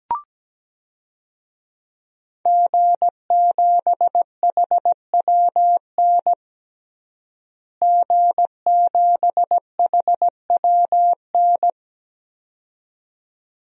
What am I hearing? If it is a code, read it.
Morse code, heard as a keyed tone, about 17 wpm, 700 Hz. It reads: G7HWN G7HWN
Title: DXCC-100 Call Signs - 5 Characters - Repeated 2x - 17wpm